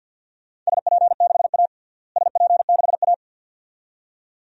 Morse code, heard as a keyed tone, about 50 wpm, 700 Hz. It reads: SP6A SP6A
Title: DXCC-100 Call Signs - 4 Characters - Repeated 2x - 50wpm